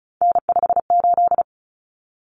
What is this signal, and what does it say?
Morse code, heard as a keyed tone, about 34 wpm, 700 Hz. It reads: N58
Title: Common Cut Numbers - 3-Digit 35wpm